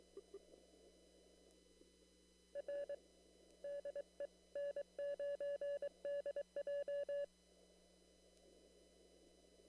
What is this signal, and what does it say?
Morse code, heard as a keyed tone, about 23 wpm, 570 Hz. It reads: R DE N9DJ
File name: CW / kt5j wpx cw-op-k5pi